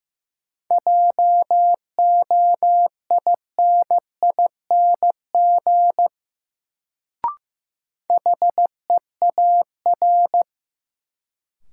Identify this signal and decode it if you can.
Morse code, heard as a keyed tone, about 15 wpm, 700 Hz. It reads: JOINING HEAR
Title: Sets of 5 Words - New Word Drill - 15wpm